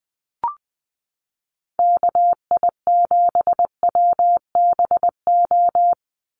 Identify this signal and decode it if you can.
Morse code, heard as a keyed tone, about 20 wpm, 700 Hz. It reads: KI7WBO
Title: USA Call Signs 20wpm